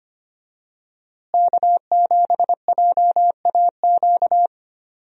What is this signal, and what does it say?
Morse code, heard as a keyed tone, about 25 wpm, 700 Hz. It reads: K7JAQ
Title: USA Call Signs - 5 Characters - Repeated 2x - 25wpm